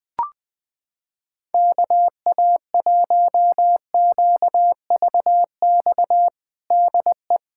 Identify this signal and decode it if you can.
Morse code, heard as a keyed tone, about 20 wpm, 700 Hz. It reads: KA1QVX DE
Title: USA Callsign DE Callsign 20wpm